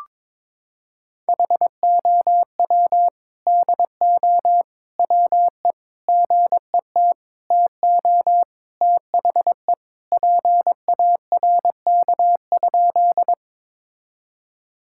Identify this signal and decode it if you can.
Morse code, heard as a keyed tone, about 22 wpm, 700 Hz. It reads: HOW DO WE GET TO THE PARK?
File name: practice / 22wpm / Sentences with 4-Word Phrases - 22wpm